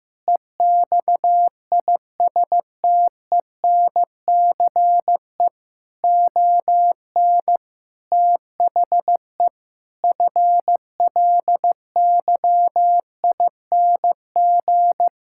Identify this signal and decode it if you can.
Morse code, heard as a keyed tone, about 15 wpm, 700 Hz. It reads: EXISTENCE ON THE FLYING
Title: Pangram Challenge v2 - 15wpm